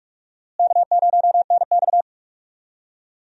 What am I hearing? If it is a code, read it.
Morse code, heard as a keyed tone, about 46 wpm, 700 Hz. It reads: K0NX
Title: USA Call Signs - 4 Characters - Repeated 2x - 45wpm